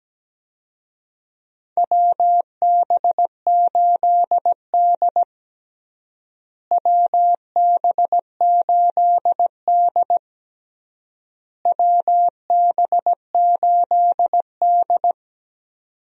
Morse code, heard as a keyed tone, about 17 wpm, 700 Hz. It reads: WB8D WB8D WB8D
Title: USA Call Signs - 4 Characters - Repeated 3x - 17wpm